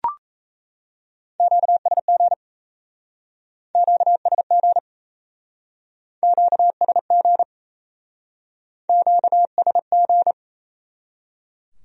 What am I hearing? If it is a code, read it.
Morse code, heard as a keyed tone, about 38 wpm, 700 Hz. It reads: QSG QSG QSG QSG